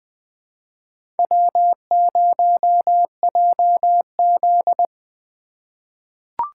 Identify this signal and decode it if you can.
Morse code, heard as a keyed tone, about 20 wpm, 700 Hz. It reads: W0JZ
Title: USA Call Signs - 4 Characters - 3x Character Spacing - 20wpm